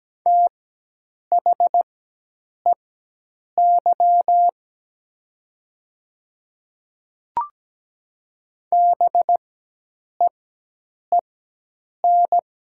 Morse code, heard as a keyed tone, about 17 wpm, spaced as for 8 wpm, 700 Hz. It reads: THEY BEEN
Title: 4x Character Spacing - Top 100 Words - 17wpm